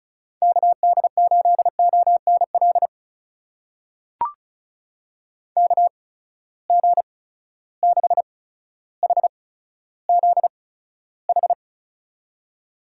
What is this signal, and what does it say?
Morse code, heard as a keyed tone, about 36 wpm, spaced as for 15 wpm, 700 Hz. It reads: KD8ONL KG6HZH
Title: USA Call Signs - 6 Characters - 8x Character Spacing - 35wpm